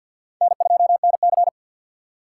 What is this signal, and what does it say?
Morse code, heard as a keyed tone, about 52 wpm, 700 Hz. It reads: N2NC
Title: K1USN SST QSO Practice - 50wpm